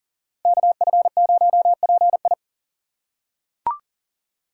Morse code, heard as a keyed tone, about 40 wpm, 700 Hz. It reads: KF0PI
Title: USA Call Signs - 5 Characters - 4x Character Spacing - 40wpm